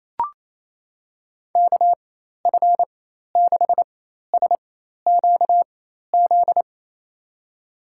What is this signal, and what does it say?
Morse code, heard as a keyed tone, about 28 wpm, spaced as for 13 wpm, 700 Hz. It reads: KF6SQZ
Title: USA Call Signs - 6 Characters - 4x Character Spacing - 28wpm